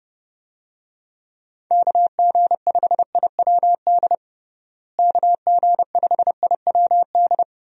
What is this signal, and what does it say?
Morse code, heard as a keyed tone, about 30 wpm, 700 Hz. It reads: KG5IWD KG5IWD
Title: USA Call Signs - 6 Characters - Repeated 2x - 30wpm